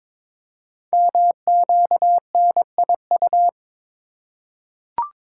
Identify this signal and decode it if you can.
Morse code, heard as a keyed tone, about 22 wpm, 700 Hz. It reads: MQNIU
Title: Code Groups - 5 Letters - 22wpm